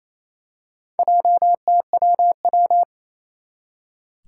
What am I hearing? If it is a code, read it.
Morse code, heard as a keyed tone, about 28 wpm, 700 Hz. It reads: JTWW
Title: Four Challenging Letter-Number Combinations 28wpm